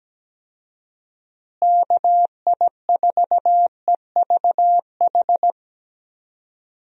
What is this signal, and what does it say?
Morse code, heard as a keyed tone, about 17 wpm, 700 Hz. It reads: KI4EVH
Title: USA Call Signs - 6 Characters - Repeated 3x - 17wpm